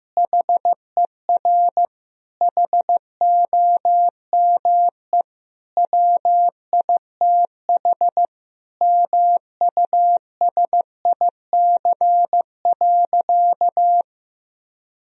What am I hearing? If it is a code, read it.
Morse code, heard as a keyed tone, about 15 wpm, 700 Hz. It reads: HER HOME WITH MUSIC.
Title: Sentences from Top 1000 Words - Review Part 1 - 15wpm